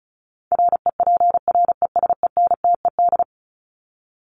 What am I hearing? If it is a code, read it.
Morse code, heard as a keyed tone, about 34 wpm, 700 Hz. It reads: REPRESENTED
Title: ED Suffix - Top 100 Words with Base Words 35wpm